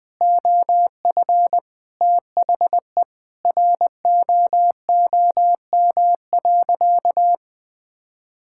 Morse code, heard as a keyed tone, about 19 wpm, 700 Hz. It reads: OF THE ROOM.
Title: Sentences with 5-Word Phrases - 20wpm